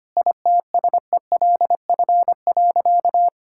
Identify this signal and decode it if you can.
Morse code, heard as a keyed tone, about 25 wpm, 700 Hz. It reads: ITSELF.
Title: Sentences from Top 1000 Words - Review Part 5 - 25wpm